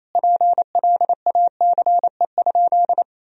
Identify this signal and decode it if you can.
Morse code, heard as a keyed tone, about 27 wpm, 700 Hz. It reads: PLACE?
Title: Sentences from Top 200 Words and Abbreviations 28wpm